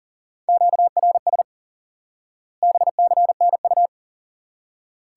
Morse code, heard as a keyed tone, about 40 wpm, 700 Hz. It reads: QRS BCNU
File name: IQE Course 3 - Lesson 89 - YAGI - Two Elements - 40wpm